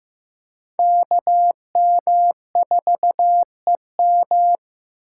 Morse code, heard as a keyed tone, about 15 wpm, 700 Hz. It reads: KM4EM
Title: USA Call Signs - 5 Characters - 8x Character Spacing - 15wpm